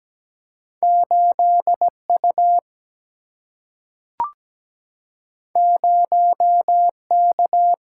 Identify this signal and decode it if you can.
Morse code, heard as a keyed tone, about 17 wpm, 700 Hz. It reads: 8U 0K
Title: Single-Letter Single-Number 17wpm